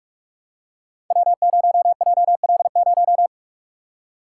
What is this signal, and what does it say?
Morse code, heard as a keyed tone, about 44 wpm, 700 Hz. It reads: W0JL0
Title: Code Groups - 5 Letter:Numbers - 45wpm